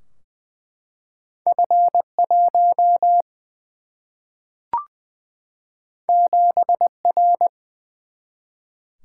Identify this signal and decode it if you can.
Morse code, heard as a keyed tone, about 20 wpm, 700 Hz. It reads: F 1 7 R
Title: Single-Letter Single-Number 20wpm